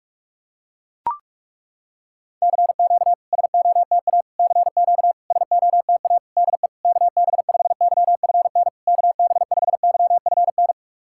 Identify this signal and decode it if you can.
Morse code, heard as a keyed tone, about 41 wpm, 690 Hz. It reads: CQ SOTA CQ SOTA DE KB5YFN KB5YFN
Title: SOTA QSO Practice 45wpm